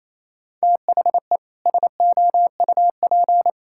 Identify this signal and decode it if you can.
Morse code, heard as a keyed tone, about 27 wpm, 700 Hz. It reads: THE SOUP
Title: Sentences from Top 1000 Words - Review Part 6 - 28wpm